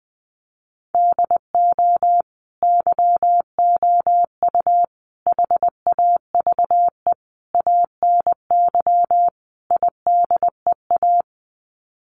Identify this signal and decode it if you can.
Morse code, heard as a keyed tone, about 20 wpm, 700 Hz. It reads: DO YOU HAVE ANY IDEA
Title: Sets of 5 Words Encore 20wpm